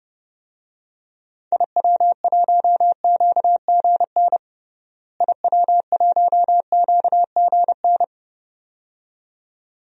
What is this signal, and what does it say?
Morse code, heard as a keyed tone, about 30 wpm, 700 Hz. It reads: IW1QGN IW1QGN
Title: DXCC-100 Call Signs - 6 Characters - Repeated 2x - 30wpm